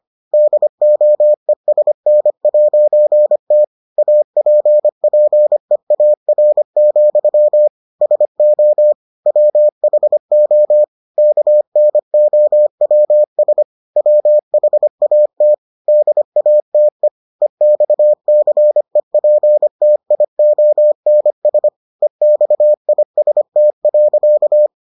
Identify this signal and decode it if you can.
Morse code, heard as a keyed tone, about 25 wpm, 600 Hz. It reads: DOESN'T APPEAR, SO WHO KNOWS WHAT DATE EXCEPTIONS EXIST.